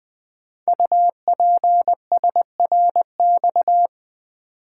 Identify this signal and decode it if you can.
Morse code, heard as a keyed tone, about 20 wpm, 700 Hz. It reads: UPSRX
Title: Code Groups - 5 Letters - 20wpm